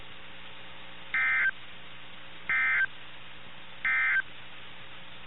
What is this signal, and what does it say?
Emergency Alert System (SAME) tones, heard as harsh buzzing bursts. It is heard from 1.0 to 4.3 s.
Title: NOAA Weather Radio - KIH61